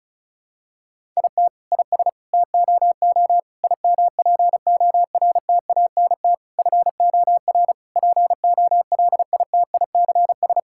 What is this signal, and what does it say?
Morse code, heard as a keyed tone, about 36 wpm, 700 Hz. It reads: IT IS TOO IMPORTANT FOR POLITICS
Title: Sentences from Top 1000 Words - Review Part 6 - 35wpm